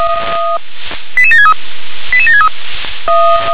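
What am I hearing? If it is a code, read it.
Five-tone selective call (ZVEI), heard as quick runs of five tones, tones of about 70 ms. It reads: 88763 88763
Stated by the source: Five-tone ZVEI selective call received on an SDR.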